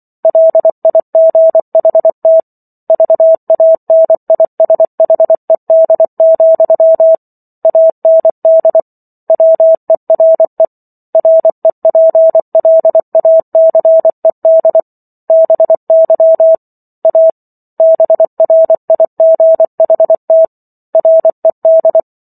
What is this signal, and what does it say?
Morse code, heard as a keyed tone, about 24 wpm, 650 Hz. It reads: LIGHT VANISHED, AND WERE REPLACED BY A BRIGHT RED
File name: Chapter 15 - 24 WPM